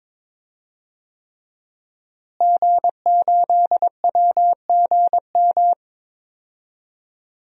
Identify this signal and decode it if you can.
Morse code, heard as a keyed tone, about 22 wpm, 700 Hz. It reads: G8WGM
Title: DXCC-100 Call Signs - 5 Characters - Repeated 3x - 22wpm